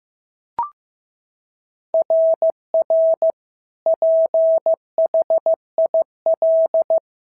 Morse code, heard as a keyed tone, about 15 wpm, 650 Hz. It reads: RR PHIL
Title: SOTA QSO Practice 15wpm